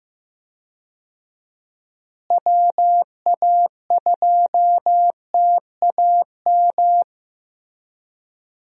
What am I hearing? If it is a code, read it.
Morse code, heard as a keyed tone, about 15 wpm, 700 Hz. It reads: WA2TAM
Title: USA Call Signs - 6 Characters - Repeated 3x - 15wpm